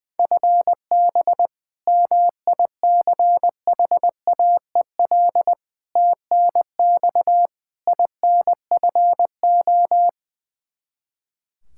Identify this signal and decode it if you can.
Morse code, heard as a keyed tone, about 21 wpm, 700 Hz. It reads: FB MICHAEL TNX INFO
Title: Practice QSO - Exchange 3 - More Info - 20wpm